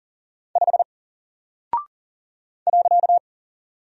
Morse code, heard as a keyed tone, about 40 wpm, 700 Hz. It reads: 5 .
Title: Single Character (Letter:Number:Punctuation:Prosign) Alt 40wpm